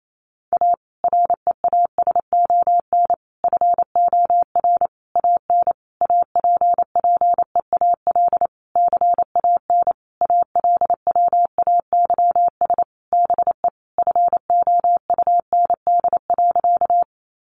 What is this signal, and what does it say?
Morse code, heard as a keyed tone, about 28 wpm, 700 Hz. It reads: A REASON FOR AN APPEAL CAN ALWAYS BE FOUND.
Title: Sentences from Top 1300 Words 28wpm